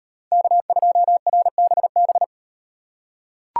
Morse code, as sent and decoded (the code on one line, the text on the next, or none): -.- ..--- .-. -... -...
K2RBB